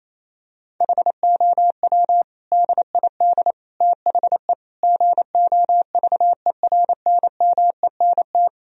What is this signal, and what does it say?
Morse code, heard as a keyed tone, about 28 wpm, 700 Hz. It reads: HOW DID THE GOVERNMENT
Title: Sentences from Top 1000 Words - Review Part 4 - 28wpm